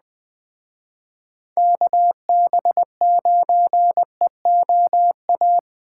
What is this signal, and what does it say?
Morse code, heard as a keyed tone, about 20 wpm, 700 Hz. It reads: KB9EOA
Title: Speed Racing - USA 2x3 Call Signs 17wpm